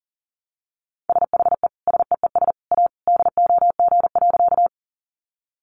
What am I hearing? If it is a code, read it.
Morse code, heard as a keyed tone, about 37 wpm, 700 Hz. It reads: SHE SEES A DOG.
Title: Easy Phrases 40wpm